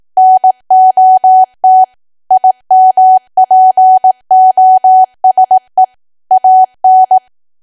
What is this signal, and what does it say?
Morse code, heard as a keyed tone, about 18 wpm, 750 Hz. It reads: NOT IMPOSE AN
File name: cwa / W1AW-Level2 / 18WPM